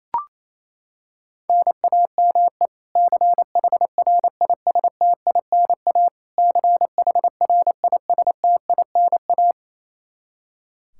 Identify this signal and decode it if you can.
Morse code, heard as a keyed tone, about 27 wpm, 700 Hz. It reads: NAME CHRISTINA CHRISTINA
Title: K1USN SST QSO Practice 28wpm